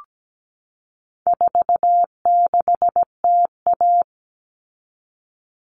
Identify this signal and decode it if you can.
Morse code, heard as a keyed tone, about 17 wpm, 700 Hz. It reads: 46TA
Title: All Cut Numbers - 4-Digit 17wpm